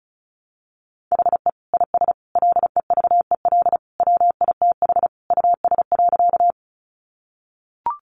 Morse code, heard as a keyed tone, about 34 wpm, 700 Hz. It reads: HE IS LEVEL WITH US.
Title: Sentences from Top 700 Words 35wpm